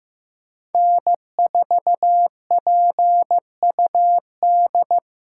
Morse code, heard as a keyed tone, about 15 wpm, 700 Hz. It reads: N4PUD
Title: USA Call Signs - 5 Characters - 2x Character Spacing - 15wpm